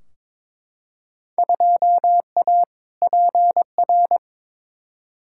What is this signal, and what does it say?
Morse code, heard as a keyed tone, about 22 wpm, 700 Hz. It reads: 2A PR
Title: Field Day Exchanges Encore - 22wpm